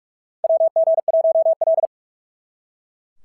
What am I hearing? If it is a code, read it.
Morse code, heard as a keyed tone, about 42 wpm, 690 Hz. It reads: WG1L
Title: SOTA QSO Practice 45wpm